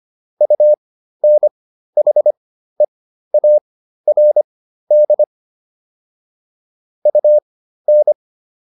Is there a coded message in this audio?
Yes: Morse code from 0.4 to 8.1 s, heard as a keyed tone, about 25 wpm, spaced as for 13 wpm, 600 Hz. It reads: UNHEARD UN